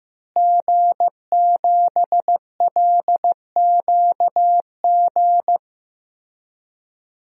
Morse code, heard as a keyed tone, about 15 wpm, 700 Hz. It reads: G7LQG
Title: DXCC-100 Call Signs - 5 Characters - Repeated 2x - 15wpm